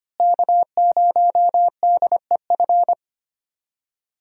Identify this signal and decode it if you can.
Morse code, heard as a keyed tone, about 25 wpm, 700 Hz. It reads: K0DEF
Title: USA Call Signs - 5 Characters - Repeated 2x - 25wpm